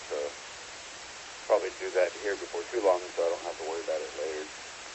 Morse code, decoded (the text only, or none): none